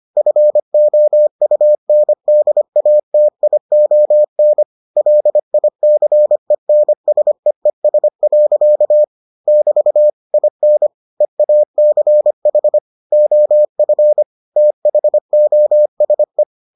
Morse code, heard as a keyed tone, about 25 wpm, 600 Hz. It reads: FOUNDATION LICENSEES. <BT> IN EACH OF THOSE